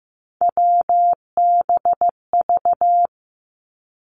Morse code, heard as a keyed tone, about 15 wpm, 700 Hz. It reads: WBV
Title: Three Challenging Letter-Number Combinations 15wpm